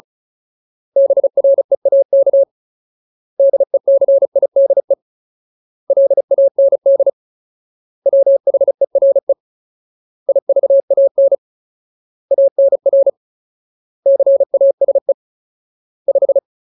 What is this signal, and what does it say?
Morse code, heard as a keyed tone, about 34 wpm, 550 Hz. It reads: BREAK DECIDE LAND WHERE IVAN ANR CASE 5
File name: Ultimate Word List / 35WPM Ultimate